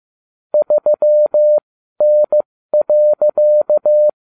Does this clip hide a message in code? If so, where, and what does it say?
Yes: Morse code from 0.5 to 4.1 s, heard as a keyed tone, about 15 wpm, spaced as for 12 wpm, 600 Hz. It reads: 3N.